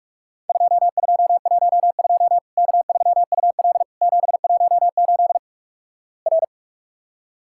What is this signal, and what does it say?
Morse code, heard as a keyed tone, about 48 wpm, 700 Hz. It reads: 2212 K3UL 718 R
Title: Worked All Europe DX Contest - QTC Exchange - 45wpm